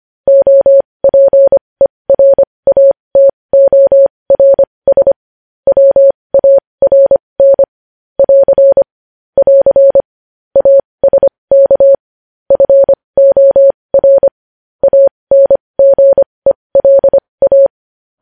Morse code, heard as a keyed tone, about 25 wpm, spaced as for 20 wpm, 550 Hz. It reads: OPERATORS WARN <AR> <AR> ASK FOR ANGELA
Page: news-headlines-in-morse-code-at-20-wpm